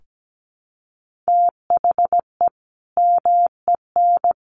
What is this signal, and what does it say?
Morse code, heard as a keyed tone, about 17 wpm, 700 Hz. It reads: THE MEN
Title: Sentences from Top 1400 Words 17wpm